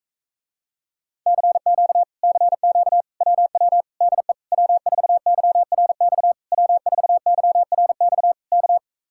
Morse code, heard as a keyed tone, about 44 wpm, 700 Hz. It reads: CQ CQ WW DE W4YRX W4YRX K